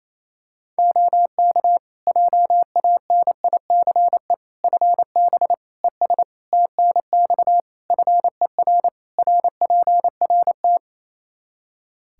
Morse code, heard as a keyed tone, about 28 wpm, 700 Hz. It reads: OK JANICE FB ES TNX FER RPRT
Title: K1USN SST QSO Practice 28wpm